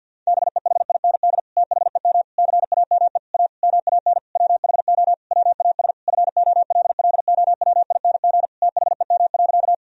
Morse code, heard as a keyed tone, about 50 wpm, 700 Hz. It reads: BEHIND THEM CAME A MAN WHO WAS FOLLOWING THEM.
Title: Sentences from Top 1000 Words - Review Part 4 - 50wpm